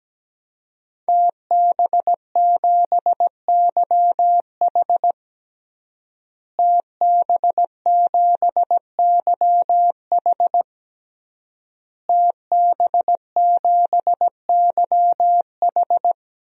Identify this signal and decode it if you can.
Morse code, heard as a keyed tone, about 17 wpm, 700 Hz. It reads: TB7YH TB7YH TB7YH
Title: DXCC-100 Call Signs - 5 Characters - Repeated 3x - 17wpm